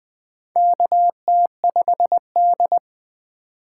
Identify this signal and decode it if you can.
Morse code, heard as a keyed tone, about 20 wpm, 700 Hz. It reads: KT5D
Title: USA Call Signs - 4 Characters - 2x Character Spacing - 20wpm